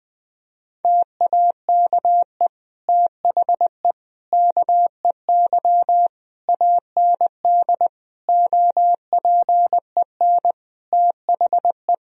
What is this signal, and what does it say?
Morse code, heard as a keyed tone, about 20 wpm, 700 Hz. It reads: TAKE THE KEY AND OPEN THE
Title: Sentences from Top 1000 Words - Review Part 4 - 20wpm